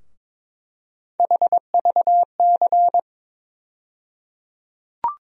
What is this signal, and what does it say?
Morse code, heard as a keyed tone, about 22 wpm, 700 Hz. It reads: HVC
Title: USA Call Sign Postfixes - 3 Letters 22wpm